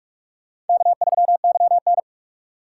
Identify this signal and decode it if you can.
Morse code, heard as a keyed tone, about 43 wpm, 700 Hz. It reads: K3YN
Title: USA Call Signs - 4x Character Spacing - 45wpm